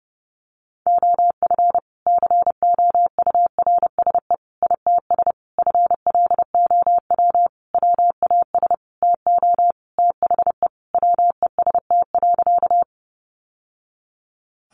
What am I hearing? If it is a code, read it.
Morse code, heard as a keyed tone, about 28 wpm, 700 Hz. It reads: OF COURSE ITS FLOW WAS TO THE WEST.
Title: Sentences from Top 1200 Words 30wpm